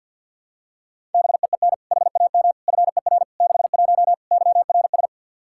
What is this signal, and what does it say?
Morse code, heard as a keyed tone, about 46 wpm, 700 Hz. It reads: BEEN HAM FER 61 YRS
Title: Practice QSO - Exchange 3 - More Info - 50wpm